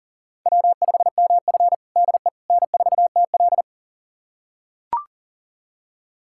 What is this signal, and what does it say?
Morse code, heard as a keyed tone, about 40 wpm, 700 Hz. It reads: W5MF DE N4TL
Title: USA Callsign DE Callsign - 4 Characters - 40wpm